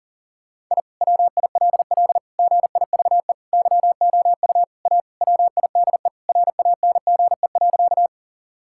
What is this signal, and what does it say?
Morse code, heard as a keyed tone, about 40 wpm, 700 Hz. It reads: I WILL GIVE YOU A WIDE RANGE.
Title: Sentences from Top 1000 Words - Review Part 6 - 40wpm